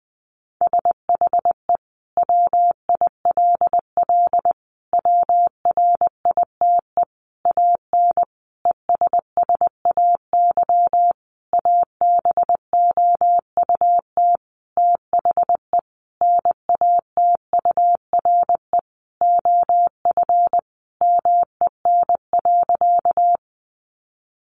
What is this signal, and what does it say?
Morse code, heard as a keyed tone, about 20 wpm, 700 Hz. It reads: SHE WILL WRITE AN ESSAY ABOUT THE NATURE OF MEN.
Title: Sentences from Top 1500 Words 20wpm